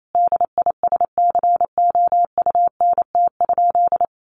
Morse code, heard as a keyed tone, about 28 wpm, 700 Hz. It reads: DISCOUNT?
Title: Sentences from Top 1200 Words 28wpm